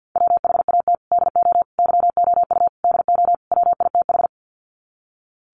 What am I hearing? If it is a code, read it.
Morse code, heard as a keyed tone, about 50 wpm, 700 Hz. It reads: WHAT DO YOU DO WITH
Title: Sets of 5 Words Encore2 50wpm